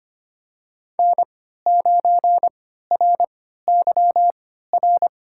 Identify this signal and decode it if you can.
Morse code, heard as a keyed tone, about 25 wpm, spaced as for 14 wpm, 700 Hz. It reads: N9RYR
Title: USA Call Signs - 5 Characters - 3x Character Spacing - 25wpm